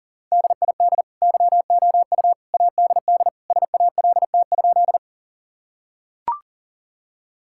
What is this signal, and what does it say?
Morse code, heard as a keyed tone, about 40 wpm, 700 Hz. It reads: DID YOU ADD SALT?
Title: Sentences from Top 1000 Words - Review Part 5 - 40wpm